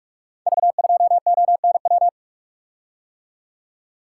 Morse code, heard as a keyed tone, about 44 wpm, 700 Hz. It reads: V2ONW
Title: Code Groups - 5 Letter:Numbers - 45wpm